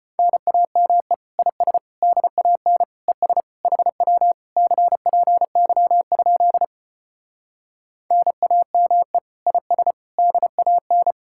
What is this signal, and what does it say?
Morse code, heard as a keyed tone, about 33 wpm, 700 Hz. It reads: NAME IS DAN ES HW CPY? NAME IS DAN